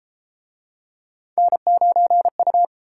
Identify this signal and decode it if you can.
Morse code, heard as a keyed tone, about 30 wpm, 700 Hz. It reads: N9U